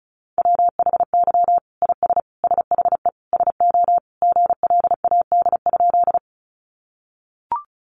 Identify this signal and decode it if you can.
Morse code, heard as a keyed tone, about 34 wpm, 700 Hz. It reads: WHY IS SHE SO GLAD?
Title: Sentences from Top 800 Words 35wpm